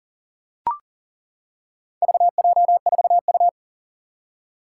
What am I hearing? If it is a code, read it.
Morse code, heard as a keyed tone, about 40 wpm, 700 Hz. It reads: VJ4U
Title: Four Challenging Letter-Number Combinations 40wpm